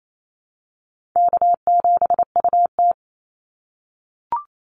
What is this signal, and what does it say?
Morse code, heard as a keyed tone, about 28 wpm, 700 Hz. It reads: K7UT
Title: Easy Call Signs - 28wpm